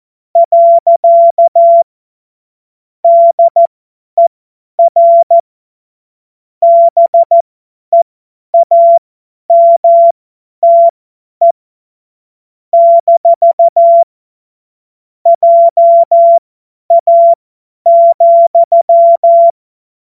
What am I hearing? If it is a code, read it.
Morse code, heard as a keyed tone, about 14 wpm, spaced as for 10 wpm, 680 Hz. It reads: . DER BEAMTE - JA,
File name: Witz-14-Oktober-2025-50-F70